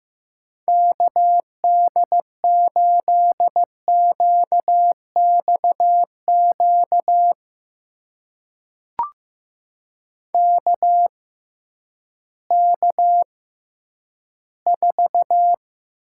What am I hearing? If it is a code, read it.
Morse code, heard as a keyed tone, about 15 wpm, spaced as for 10 wpm, 700 Hz. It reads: KD8QXQ KK4
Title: USA Call Signs - 6 Characters - 6x Character Spacing - 15wpm